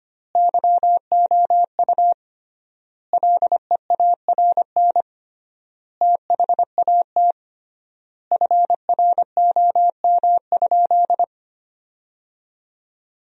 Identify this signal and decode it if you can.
Morse code, heard as a keyed tone, about 25 wpm, 700 Hz. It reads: YOU LEARN THAT FROM?
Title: 3x Word-Spacing - Sentences from Top 300 Words - 25wpm